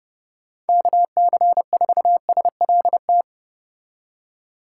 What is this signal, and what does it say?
Morse code, heard as a keyed tone, about 31 wpm, 700 Hz. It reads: KC4SLT